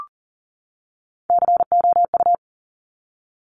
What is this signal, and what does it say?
Morse code, heard as a keyed tone, about 39 wpm, 700 Hz. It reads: COU
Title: Top Three Letter Combinations in Words 40wpm